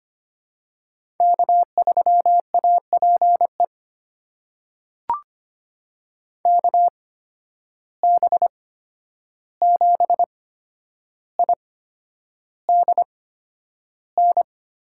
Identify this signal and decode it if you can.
Morse code, heard as a keyed tone, about 25 wpm, spaced as for 10 wpm, 700 Hz. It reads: K3APE KB7IDN
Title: USA Call Signs - 8x Character Spacing - 25wpm